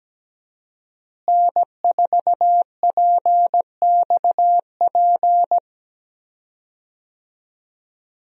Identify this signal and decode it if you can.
Morse code, heard as a keyed tone, about 17 wpm, 700 Hz. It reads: N4PXP
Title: USA Call Signs - 5 Characters - Repeated 3x - 17wpm